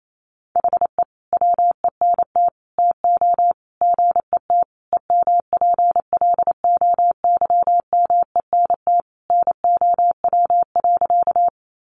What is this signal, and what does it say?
Morse code, heard as a keyed tone, about 28 wpm, 700 Hz. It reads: HE WENT TO GET EMPLOYMENT NOW.
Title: Sentences from Top 1500 Words 28wpm